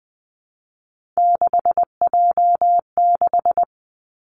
Morse code, heard as a keyed tone, about 20 wpm, 700 Hz. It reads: 6J6
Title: Three Challenging Letter-Number Combinations 20wpm